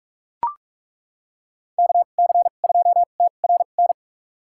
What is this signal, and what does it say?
Morse code, heard as a keyed tone, about 41 wpm, spaced as for 31 wpm, 700 Hz. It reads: KC2TRN
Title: USA Call Signs - 6 Characters - 2x Character Spacing - 45wpm